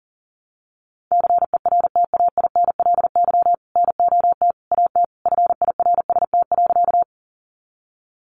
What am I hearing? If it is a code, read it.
Morse code, heard as a keyed tone, about 40 wpm, 700 Hz. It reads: CERTAINLY NOT AT FIRST.
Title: Sentences from Top 900 Words 40wpm